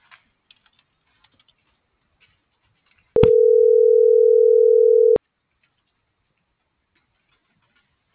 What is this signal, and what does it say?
Ringback tone (North America), 440 + 480 Hz.
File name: CENTRAL GOMSA / out aylin.bazan